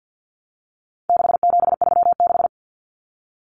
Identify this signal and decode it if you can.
Morse code, heard as a keyed tone, about 50 wpm, 700 Hz. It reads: 6736
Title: Common Cut Numbers - 4-Digit 50wpm